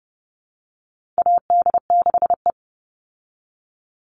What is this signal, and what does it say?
Morse code, heard as a keyed tone, about 30 wpm, 700 Hz. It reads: AD6E
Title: Call Signs Repeated 2x - 30wpm